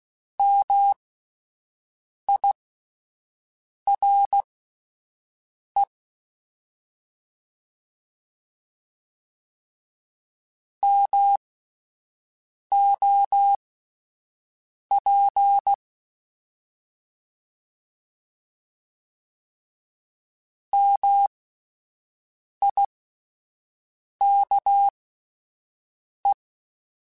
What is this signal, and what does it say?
Morse code, heard as a keyed tone, about 16 wpm, spaced as for 5 wpm, 800 Hz. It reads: MIRE MOP MIKE